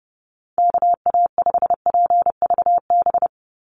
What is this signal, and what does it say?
Morse code, heard as a keyed tone, about 30 wpm, 700 Hz. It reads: KA5PVB
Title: NA SOTA Chaser Callsigns 30wpm